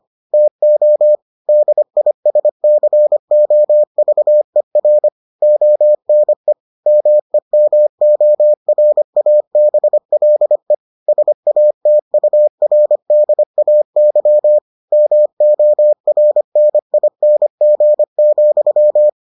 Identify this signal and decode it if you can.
Morse code, heard as a keyed tone, about 25 wpm, 600 Hz. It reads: TO DISCOVER ONE MEMORABLE SATURDAY MORNING,